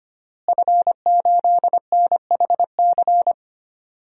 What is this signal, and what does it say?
Morse code, heard as a keyed tone, about 25 wpm, 700 Hz. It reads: F8NHC
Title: DXCC-100 Call Signs - 5 Characters - Repeated 3x - 25wpm